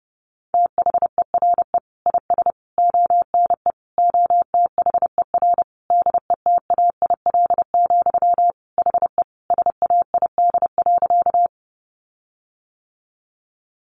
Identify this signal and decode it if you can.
Morse code, heard as a keyed tone, about 28 wpm, 700 Hz. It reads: THERE IS ONE OTHER DETAIL, HE SAID.
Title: Sentences from Top 1500 Words 30wpm